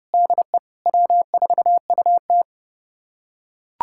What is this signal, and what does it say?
Morse code, heard as a keyed tone, about 32 wpm, 700 Hz. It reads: DE W4UT
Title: USA Callsign DE Callsign - 4 Characters - 30wpm